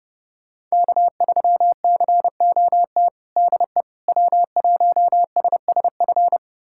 Morse code, heard as a keyed tone, about 32 wpm, 700 Hz. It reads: K3COT DE W1SSF
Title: USA Callsign DE Callsign 30wpm